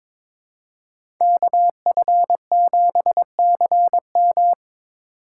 Morse code, heard as a keyed tone, about 22 wpm, 700 Hz. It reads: KF7CM
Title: USA Call Signs - 5 Characters - 8x Character Spacing - 22wpm